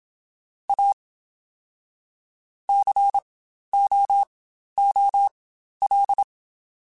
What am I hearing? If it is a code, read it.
Morse code, heard as a keyed tone, about 27 wpm, spaced as for 12 wpm, 800 Hz. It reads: A COOL
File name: Spaced Characters / Spaced Characters-mp3s / Weather4 14 sp @25